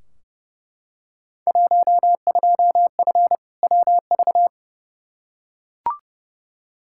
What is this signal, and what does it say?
Morse code, heard as a keyed tone, about 30 wpm, 700 Hz. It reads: 12F WV
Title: Field Day Exchanges Encore - 30wpm